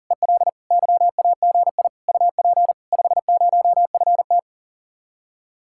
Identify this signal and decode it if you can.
Morse code, heard as a keyed tone, about 51 wpm, 700 Hz. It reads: EL YAGI UP 50FT